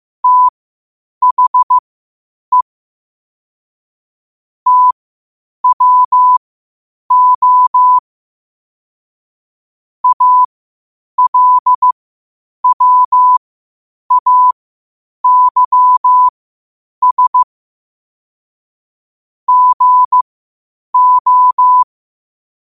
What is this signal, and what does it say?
Morse code, heard as a keyed tone, about 15 wpm, spaced as for 8 wpm, 1000 Hz. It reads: THE TWO ALWAYS GO
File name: qotdfiles / 8wpm